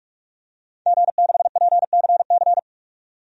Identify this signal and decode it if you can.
Morse code, heard as a keyed tone, about 42 wpm, 700 Hz. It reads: G6PCC
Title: DXCC-100 Call Signs - 5 Characters - 45wpm